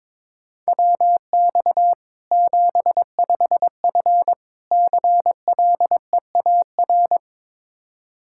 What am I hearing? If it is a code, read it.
Morse code, heard as a keyed tone, about 22 wpm, 700 Hz. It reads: WX 75F CLEAR
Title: K1USN SST QSO Practice 22wpm